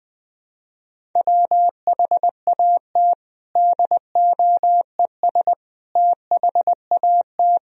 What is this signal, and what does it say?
Morse code, heard as a keyed tone, about 20 wpm, 700 Hz. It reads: WHAT DOES THAT
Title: Sentences from Top 1000 Words - Review Part 1 - 20wpm